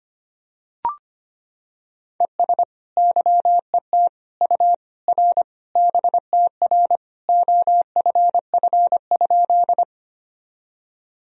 Morse code, heard as a keyed tone, about 25 wpm, 700 Hz. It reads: ES YET U R BTR OFF?
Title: Sentences from Top 500 Words and Abbreviations 25wpm